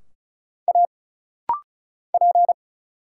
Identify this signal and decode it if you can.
Morse code, heard as a keyed tone, about 36 wpm, 700 Hz. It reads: A P
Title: Single Letter-Number - Rapid-Fire 35wpm